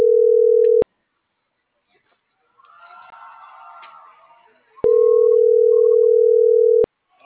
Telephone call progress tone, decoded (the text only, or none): ringback tone